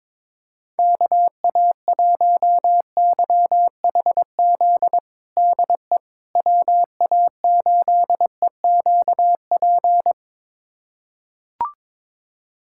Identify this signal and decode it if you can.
Morse code, heard as a keyed tone, about 22 wpm, 700 Hz. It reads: KA1YHZ DE WA8EQP
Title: USA Callsign DE Callsign - 6 Characters - 22wpm